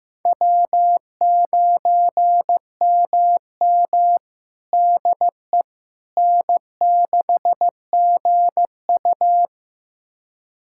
Morse code, heard as a keyed tone, about 15 wpm, 700 Hz. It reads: W9MM DE N6GU
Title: USA Callsign DE Callsign - 4 Characters - 15wpm